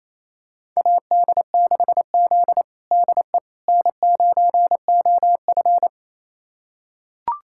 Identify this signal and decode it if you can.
Morse code, heard as a keyed tone, about 28 wpm, 700 Hz. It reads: AD6Z DE N9OF
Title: USA Callsign DE Callsign - 4 Characters - 28wpm